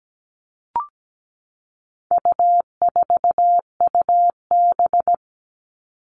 Morse code, heard as a keyed tone, about 17 wpm, 700 Hz. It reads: U4UB
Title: All Cut Numbers - 4-Digit 17wpm